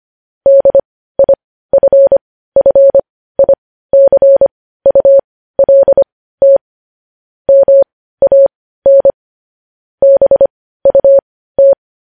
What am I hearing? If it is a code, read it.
Morse code, heard as a keyed tone, about 25 wpm, spaced as for 15 wpm, 550 Hz. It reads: DIFFICULT MAN BUT
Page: news headlines in morse code at 15 wpm